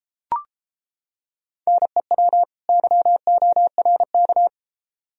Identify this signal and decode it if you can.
Morse code, heard as a keyed tone, about 33 wpm, 700 Hz. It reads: NEW YORK